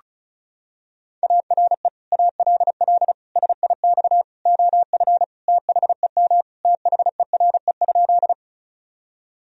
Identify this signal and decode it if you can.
Morse code, heard as a keyed tone, about 36 wpm, 700 Hz. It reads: ARE ALL SIX OF THEM THERE?
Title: Sentences from Top 1000 Words - Review Part 2 - 35wpm